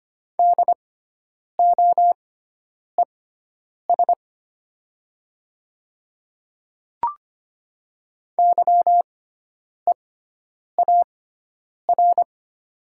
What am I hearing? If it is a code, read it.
Morse code, heard as a keyed tone, about 25 wpm, spaced as for 9 wpm, 700 Hz. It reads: DOES YEAR
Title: 6x Character Spacing - Top 300 Words - 25wpm